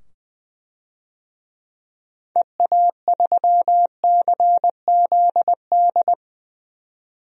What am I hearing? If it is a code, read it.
Morse code, heard as a keyed tone, about 20 wpm, 700 Hz. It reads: EA3CZD
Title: DXCC-100 Call Signs - 6 Characters - 20wpm